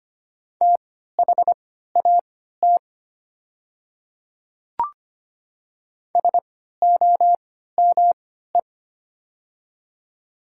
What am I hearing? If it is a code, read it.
Morse code, heard as a keyed tone, about 25 wpm, spaced as for 14 wpm, 700 Hz. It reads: THAT SOME
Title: 3x Character Spacing - Top 200 Words - 25wpm